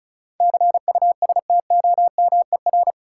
Morse code, heard as a keyed tone, about 34 wpm, 700 Hz. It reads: CUSTOMER
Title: Sentences from Top 1100 Words - New Word Drill Alt - 35wpm